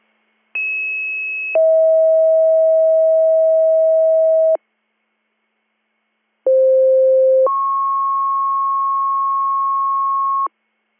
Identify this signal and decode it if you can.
Two-tone page, heard as a one-second tone, then a three-second tone, about 2600 then 640 Hz, about 520 then 1090 Hz. It is heard from 0.5 to 10.5 s.